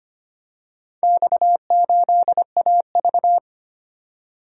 Morse code, heard as a keyed tone, about 25 wpm, 700 Hz. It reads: X8AV
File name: practice / 25wpm / Four Letter-Number Combinations - 25wpm